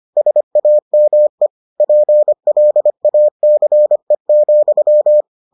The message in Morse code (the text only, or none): SAME PLACE,